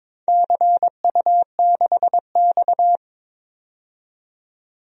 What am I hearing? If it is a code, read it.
Morse code, heard as a keyed tone, about 22 wpm, 700 Hz. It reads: CU6X
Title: DXCC-100 Call Signs - 4 Characters - 22wpm